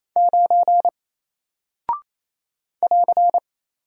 Morse code, heard as a keyed tone, about 27 wpm, 700 Hz. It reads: 9 <AR>
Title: Single Character (Letter:Number:Punctuation:Prosign) 28wpm